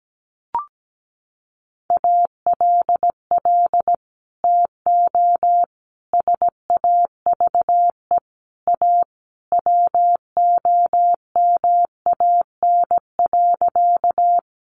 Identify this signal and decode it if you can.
Morse code, heard as a keyed tone, about 17 wpm, 700 Hz. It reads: ALL TO SAVE A WOMAN.
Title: Sentences from Top 1100 Words 17wpm